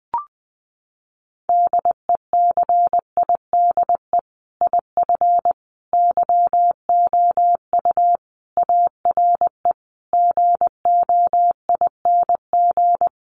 Morse code, heard as a keyed tone, about 19 wpm, 700 Hz. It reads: DECIDE IF YOU ARE GOING
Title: Sentences from Top 500 Words 20wpm